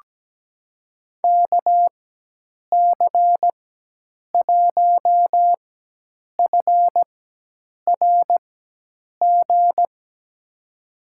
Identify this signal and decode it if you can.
Morse code, heard as a keyed tone, about 17 wpm, spaced as for 8 wpm, 700 Hz. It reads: KC1FRG
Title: USA Call Signs - 6 Characters - 4x Character Spacing - 17wpm